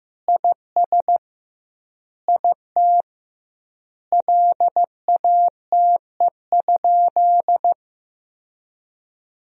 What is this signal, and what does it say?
Morse code, heard as a keyed tone, about 16 wpm, 700 Hz. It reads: IS IT LATE?
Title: 2x Word-Spacing - Sentences from Top 300 Words - 15wpm